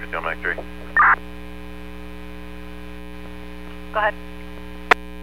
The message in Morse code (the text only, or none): none